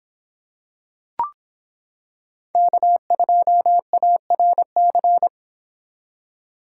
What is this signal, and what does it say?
Morse code, heard as a keyed tone, about 26 wpm, 700 Hz. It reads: K2ARC